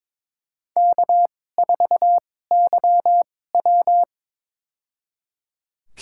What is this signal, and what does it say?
Morse code, heard as a keyed tone, about 21 wpm, spaced as for 16 wpm, 700 Hz. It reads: K4YW
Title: USA Call Signs - 4 Characters - 2x Character Spacing - 22wpm